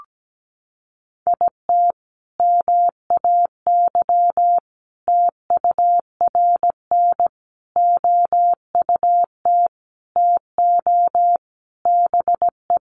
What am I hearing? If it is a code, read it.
Morse code, heard as a keyed tone, about 17 wpm, 700 Hz. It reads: IT MAY TURN OUT TO BE
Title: Sentences from Top 400 Words 17wpm